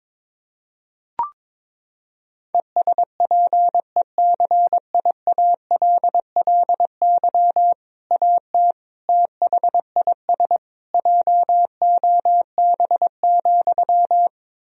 Morse code, heard as a keyed tone, about 22 wpm, 700 Hz. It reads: ESPECIALLY AT THIS JOB,
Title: Sentences from Top 1000 Words - Review Part 4 - 22wpm